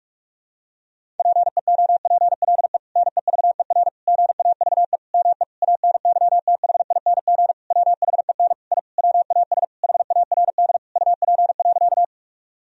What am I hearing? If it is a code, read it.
Morse code, heard as a keyed tone, about 41 wpm, 700 Hz. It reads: PEOPLE NEVER GAVE ME ANYTHING WHEN I WAS HARD UP.
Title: Sentences from Top 1000 Words - Review Part 1 - 45wpm